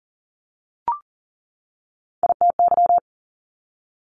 Morse code, heard as a keyed tone, about 38 wpm, 690 Hz. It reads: ITY
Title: Multi-letter Phonemes 40wpm